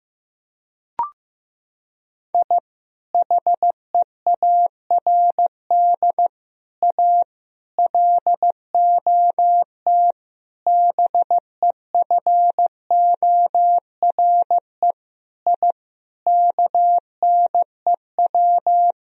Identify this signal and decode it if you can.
Morse code, heard as a keyed tone, about 15 wpm, 700 Hz. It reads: I HEARD A LOT BEFORE I KNEW
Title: Sentences from Top 1000 Words - Review Part 3 - 15wpm